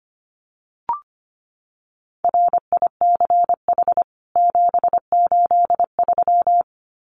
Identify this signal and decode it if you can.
Morse code, heard as a keyed tone, about 25 wpm, 700 Hz. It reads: RICH 783
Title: CWT Exchange 25wpm